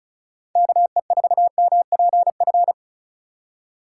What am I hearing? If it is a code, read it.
Morse code, heard as a keyed tone, about 34 wpm, 690 Hz. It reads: KE4MPF
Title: USA Call Signs - 6 Characters - 2x Character Spacing - 35wpm